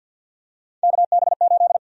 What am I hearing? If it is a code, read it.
Morse code, heard as a keyed tone, about 50 wpm, 700 Hz. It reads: KB8
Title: USA Call Sign Prefixes - Repeated 2x - 50wpm